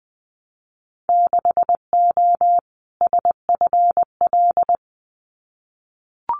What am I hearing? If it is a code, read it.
Morse code, heard as a keyed tone, about 20 wpm, 700 Hz. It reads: 6O SFL
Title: Winter Field Day Exchanges Encore 20wpm